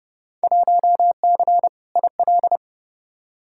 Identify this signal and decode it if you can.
Morse code, heard as a keyed tone, about 30 wpm, 700 Hz. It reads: 1C IL
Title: Field Day Exchanges - 30wpm